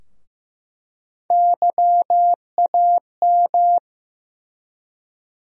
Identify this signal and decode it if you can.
Morse code, heard as a keyed tone, about 15 wpm, 700 Hz. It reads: YAM
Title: USA Call Sign Postfixes - 3 Letters 15wpm